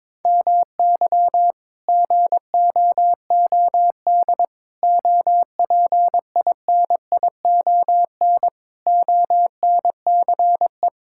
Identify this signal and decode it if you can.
Morse code, heard as a keyed tone, about 22 wpm, 700 Hz. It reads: MY GOOD OPINION ONCE
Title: Sentences from Top 1600 Words 22wpm